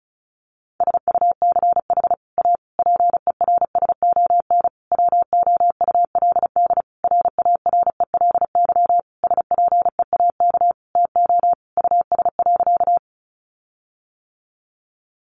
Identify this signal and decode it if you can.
Morse code, heard as a keyed tone, about 34 wpm, 700 Hz. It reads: SUCH A PERSON WOULD RARELY SPEAK TO US.
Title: Sentences from Top 1400 Words 35wpm